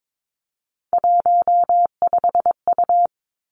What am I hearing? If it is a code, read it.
Morse code, heard as a keyed tone, about 22 wpm, 700 Hz. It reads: 15U
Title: Three Challenging Letter-Number Combinations 22wpm